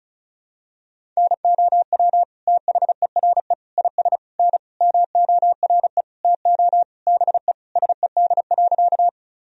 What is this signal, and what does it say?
Morse code, heard as a keyed tone, about 36 wpm, 700 Hz. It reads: NOW THERE IS N MORE TO BE SED.
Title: Sentences from Top 100 Words and Abbreviations 35wpm